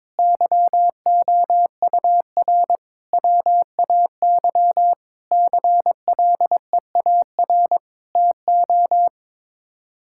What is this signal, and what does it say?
Morse code, heard as a keyed tone, about 22 wpm, 700 Hz. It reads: YOUR WAY CLEAR TO
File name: Sets of 9 Words - 22wpm